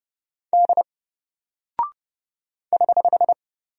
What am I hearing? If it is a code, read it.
Morse code, heard as a keyed tone, about 29 wpm, 700 Hz. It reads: D <HH>
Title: Single Character (Letter:Number:Punctuation:Prosign) 30wpm